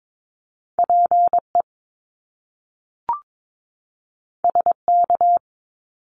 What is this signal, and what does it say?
Morse code, heard as a keyed tone, about 22 wpm, 700 Hz. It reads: PE SK
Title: Canadian Provinces and Territories 22wpm